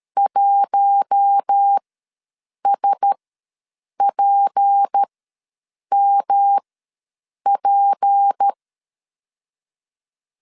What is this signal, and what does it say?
Morse code, heard as a keyed tone, about 13 wpm, spaced as for 7 wpm, 800 Hz. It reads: 1SPMP
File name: LEZIONI / 6 PKX0 / 6 P ws 12wpm 3117